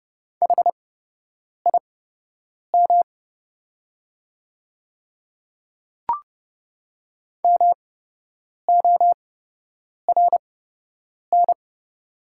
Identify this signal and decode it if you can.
Morse code, heard as a keyed tone, about 30 wpm, spaced as for 8 wpm, 690 Hz. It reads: HIM MORN
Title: 8x Character Spacing - Top 500 Words - 30wpm